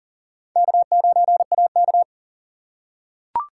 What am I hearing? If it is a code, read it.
Morse code, heard as a keyed tone, about 40 wpm, 700 Hz. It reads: K9AK
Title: USA Call Signs - 4 Characters - 3x Character Spacing - 40wpm